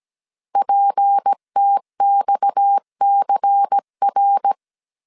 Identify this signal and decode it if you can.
Morse code, heard as a keyed tone, about 17 wpm, 800 Hz. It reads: PTXCR
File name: 16wpm